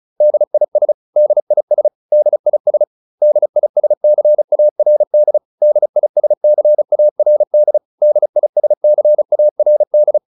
Morse code, heard as a keyed tone, about 34 wpm, 600 Hz. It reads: DIS DIS DIS DISCARD DISCARD DISCARD